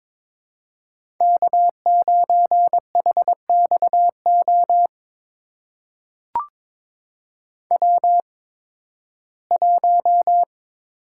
Morse code, heard as a keyed tone, about 21 wpm, spaced as for 14 wpm, 700 Hz. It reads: K9HXO W1
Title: USA Call Signs - 5 Characters - 8x Character Spacing - 22wpm